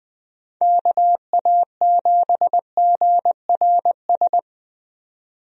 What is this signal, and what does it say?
Morse code, heard as a keyed tone, about 20 wpm, 700 Hz. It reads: KA7GRS
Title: USA Call Signs - 6 Characters - 20wpm